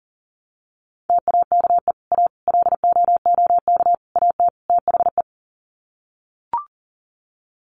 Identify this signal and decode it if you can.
Morse code, heard as a keyed tone, about 40 wpm, 700 Hz. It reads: TAKE A LOOK AT THE
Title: Sets of 5 Words 40wpm